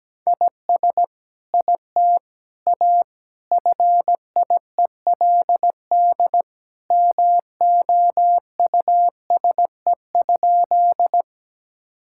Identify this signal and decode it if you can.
Morse code, heard as a keyed tone, about 17 wpm, 700 Hz. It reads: IS IT A FIELD MOUSE?
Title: Sentences from Top 1000 Words - Review Part 5 - 17wpm